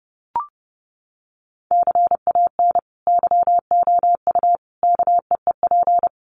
Morse code, heard as a keyed tone, about 30 wpm, 700 Hz. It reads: CAN YOU KEEP
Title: Sentences from Top 1100 Words 30wpm